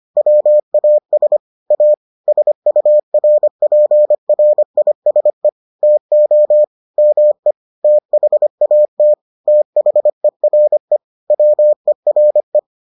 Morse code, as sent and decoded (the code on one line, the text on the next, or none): .-- .- ... / .- / ... ..- .-. .--. .-. .. ... / - --- / -- . / - .... .- - / - .... .-. / .-- . .-. .
WAS A SURPRISE TO ME THAT THERE WERE